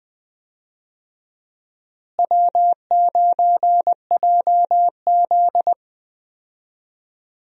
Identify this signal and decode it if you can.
Morse code, heard as a keyed tone, about 20 wpm, 700 Hz. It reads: W9JZ
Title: USA Call Signs - 4 Characters - Repeated 2x - 20wpm